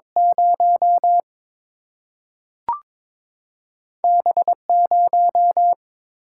Morse code, heard as a keyed tone, about 22 wpm, 700 Hz. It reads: 0 B0
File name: practice / 22wpm / Two Letter-Number Combinations - 22wpm